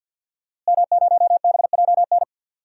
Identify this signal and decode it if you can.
Morse code, heard as a keyed tone, about 52 wpm, 700 Hz. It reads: M0BJN
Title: DXCC-100 Call Signs - 5 Characters - Repeated 3x - 50wpm